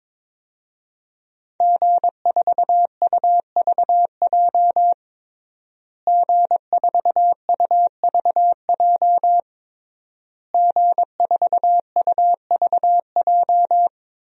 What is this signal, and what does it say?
Morse code, heard as a keyed tone, about 22 wpm, 700 Hz. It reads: G4UVJ G4UVJ G4UVJ
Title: DXCC-100 Call Signs - 5 Characters - Repeated 3x - 22wpm